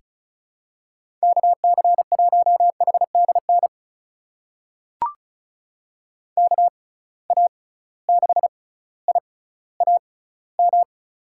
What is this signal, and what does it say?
Morse code, heard as a keyed tone, about 34 wpm, spaced as for 18 wpm, 700 Hz. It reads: KC1HDN KA6IAM
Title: USA Call Signs - 6 Characters - 6x Character Spacing - 35wpm